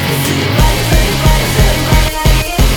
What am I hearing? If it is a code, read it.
There is no code.